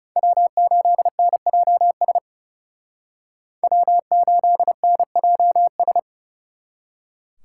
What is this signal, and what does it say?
Morse code, heard as a keyed tone, about 38 wpm, 700 Hz. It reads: W8NJS W8NJS